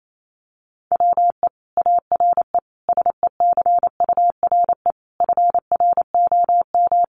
Morse code, heard as a keyed tone, about 28 wpm, 700 Hz. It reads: WE ARE SECURE FROM
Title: Sentences from Top 1200 Words 28wpm